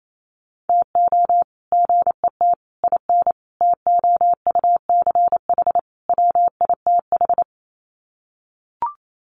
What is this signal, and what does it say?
Morse code, heard as a keyed tone, about 28 wpm, 700 Hz. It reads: TO GET IN TOUCH WITH
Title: Sets of 5 Words Encore2 28wpm